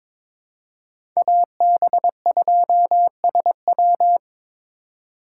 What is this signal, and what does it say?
Morse code, heard as a keyed tone, about 22 wpm, 700 Hz. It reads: AB2SW
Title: USA Call Signs - 5 Characters - 2x Character Spacing - 22wpm